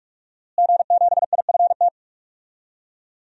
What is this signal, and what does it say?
Morse code, heard as a keyed tone, about 46 wpm, 700 Hz. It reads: G7IFT
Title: DXCC-100 Call Signs - 5 Characters - Repeated 3x - 45wpm